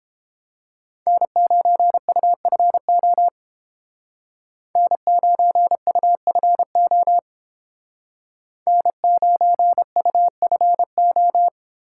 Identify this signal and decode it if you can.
Morse code, heard as a keyed tone, about 31 wpm, 700 Hz. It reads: N9UFO N9UFO N9UFO